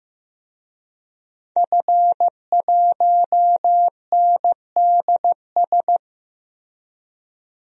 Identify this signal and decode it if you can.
Morse code, heard as a keyed tone, about 15 wpm, 700 Hz. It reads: F1NDS
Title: DXCC-100 Call Signs - 5 Characters - Repeated 3x - 15wpm